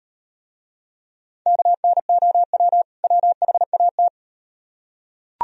Morse code, heard as a keyed tone, about 38 wpm, 700 Hz. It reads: KNOW WHAT